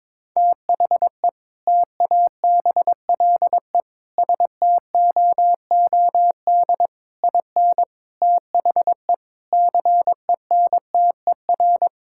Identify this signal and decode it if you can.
Morse code, heard as a keyed tone, about 22 wpm, 700 Hz. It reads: THE TABLE STOOD IN THE CENTER
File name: practice / 22wpm / Sentences with 4-Word Phrases - 22wpm